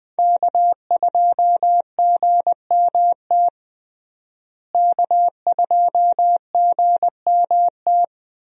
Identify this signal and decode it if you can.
Morse code, heard as a keyed tone, about 20 wpm, 700 Hz. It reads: K2GMT K2GMT
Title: USA Call Signs - 5 Characters - Repeated 3x - 20wpm